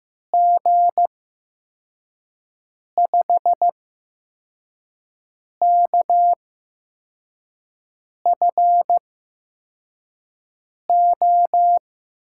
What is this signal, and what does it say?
Morse code, heard as a keyed tone, about 15 wpm, spaced as for 4 wpm, 700 Hz. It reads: G5KFO
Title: USA Call Signs - 6 Characters - 8x Character Spacing - 15wpm